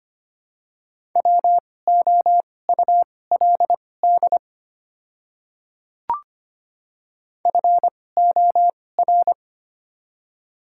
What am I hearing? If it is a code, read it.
Morse code, heard as a keyed tone, about 25 wpm, spaced as for 18 wpm, 700 Hz. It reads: WOULD FOR
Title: 2x Character Spacing - Top 500 Words - 25wpm